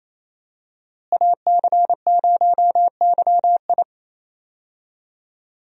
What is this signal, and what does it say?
Morse code, heard as a keyed tone, about 28 wpm, 700 Hz. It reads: AC0YI
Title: USA Call Signs - 5 Characters - Repeated 2x - 28wpm